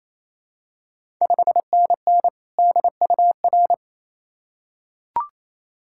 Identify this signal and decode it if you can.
Morse code, heard as a keyed tone, about 27 wpm, 690 Hz. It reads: 5NN DUR
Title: Ontario QSO Party Exchange 28wpm